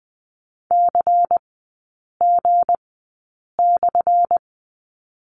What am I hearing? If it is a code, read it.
Morse code, heard as a keyed tone, about 20 wpm, spaced as for 8 wpm, 700 Hz. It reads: CG/
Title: 205 Lesson 33e - Letter j - Three Characters 20wpm